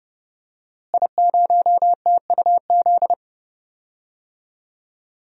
Morse code, heard as a keyed tone, about 32 wpm, 700 Hz. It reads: I0TUZ
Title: DXCC-100 Call Signs - 5 Characters - Repeated 3x - 30wpm